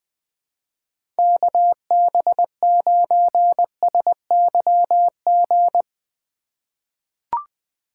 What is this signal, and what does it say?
Morse code, heard as a keyed tone, about 20 wpm, 700 Hz. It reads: KB9SYG
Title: USA Call Signs - 6 Characters - 2x Character Spacing - 20wpm